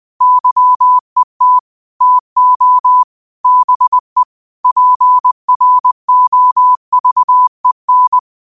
Morse code, heard as a keyed tone, about 20 wpm, 1000 Hz. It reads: YET TO BE PROVEN
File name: qotdfiles / 20wpm